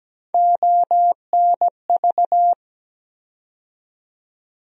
Morse code, heard as a keyed tone, about 17 wpm, 700 Hz. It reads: ONV
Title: USA Call Sign Postfixes - 3 Letters 17wpm